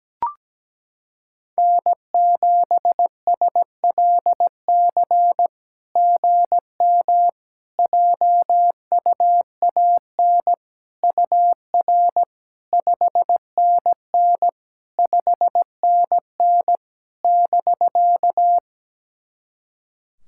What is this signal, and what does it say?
Morse code, heard as a keyed tone, about 17 wpm, 700 Hz. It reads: N7SLC GM JUAN UR 5NN 5NN <BK>
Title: SOTA QSO Practice 17wpm